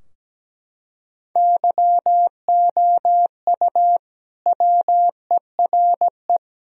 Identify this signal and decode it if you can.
Morse code, heard as a keyed tone, about 17 wpm, 700 Hz. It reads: YOU WERE
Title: Sentences from Top 1000 Words - Review Part 4 - 17wpm